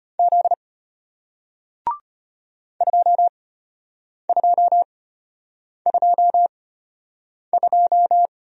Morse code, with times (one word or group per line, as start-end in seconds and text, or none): 0.2-0.5 Z
2.8-8.3 2222